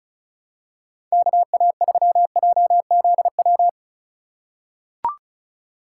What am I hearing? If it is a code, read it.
Morse code, heard as a keyed tone, about 36 wpm, 700 Hz. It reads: KA3JZW
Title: USA Call Signs - 6 Characters - 3x Character Spacing - 35wpm